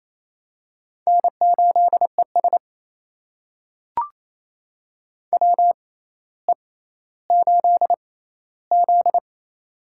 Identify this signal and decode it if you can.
Morse code, heard as a keyed tone, about 28 wpm, spaced as for 14 wpm, 700 Hz. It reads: N8ES WE8Z
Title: USA Call Signs - 4 Characters - 6x Character Spacing - 28wpm